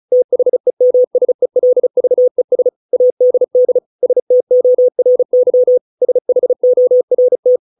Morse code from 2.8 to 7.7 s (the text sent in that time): ADD STORY SHORT